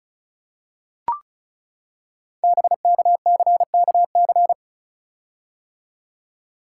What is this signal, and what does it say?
Morse code, heard as a keyed tone, about 36 wpm, 700 Hz. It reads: BKCKC
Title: Code Groups - 5 Letters - 35wpm